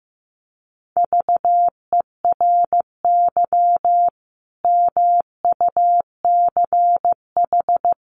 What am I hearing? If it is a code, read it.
Morse code, heard as a keyed tone, about 15 wpm, 700 Hz. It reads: VERY MUCH
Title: Sets of 5 Words 15wpm